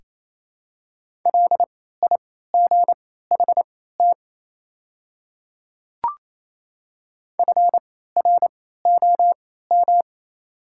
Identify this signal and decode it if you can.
Morse code, heard as a keyed tone, about 28 wpm, spaced as for 16 wpm, 700 Hz. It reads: LIGHT FROM
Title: 3x Character Spacing - Top 200 Words - 28wpm